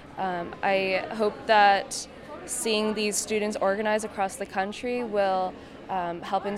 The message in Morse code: none

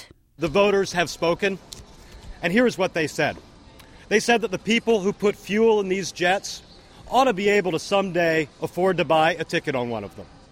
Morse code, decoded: none

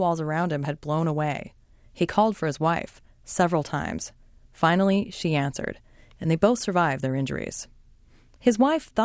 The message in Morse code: none

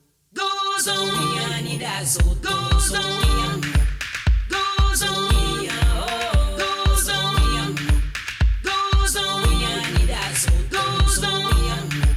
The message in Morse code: none

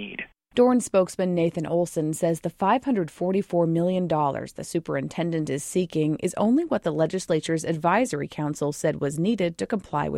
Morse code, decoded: none

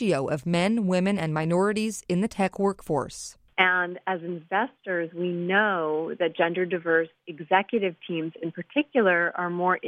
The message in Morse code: none

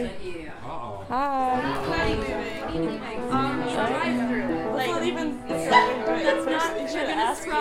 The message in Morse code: none

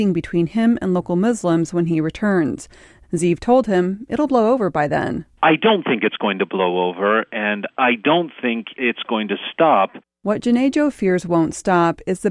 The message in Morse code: none

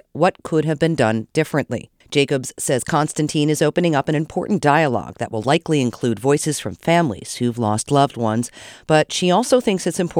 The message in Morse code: none

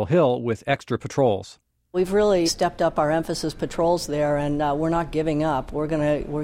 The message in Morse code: none

